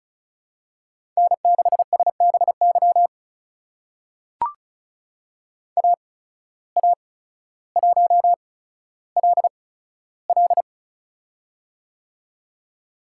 Morse code, heard as a keyed tone, about 36 wpm, spaced as for 15 wpm, 700 Hz. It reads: N6SBY AA1LL
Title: USA Call Signs - 5 Characters - 8x Character Spacing - 35wpm